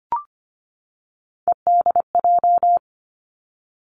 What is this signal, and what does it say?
Morse code, heard as a keyed tone, about 25 wpm, 700 Hz. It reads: EDJ
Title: Three Challenging Letter-Number Combinations 25wpm